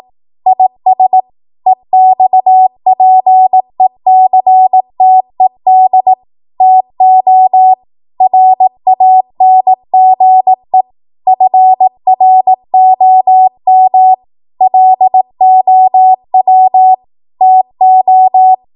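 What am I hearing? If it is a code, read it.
Morse code, heard as a keyed tone, about 18 wpm, 750 Hz. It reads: IS EXPECTED TO RANGE FROM LOW TO
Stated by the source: Code practice at 18 WPM, a transition run bridging 15 to 20 WPM.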